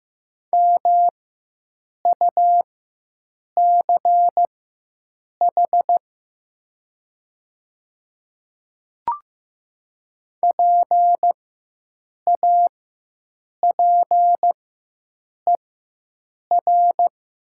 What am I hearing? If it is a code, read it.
Morse code, heard as a keyed tone, about 15 wpm, spaced as for 7 wpm, 700 Hz. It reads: MUCH PAPER
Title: 4x Character Spacing - Top 300 Words - 15wpm